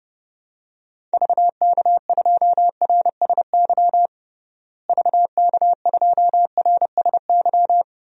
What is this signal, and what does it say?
Morse code, heard as a keyed tone, about 30 wpm, 700 Hz. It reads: VK2RSY VK2RSY
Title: DXCC-100 Call Signs - 6 Characters - Repeated 2x - 30wpm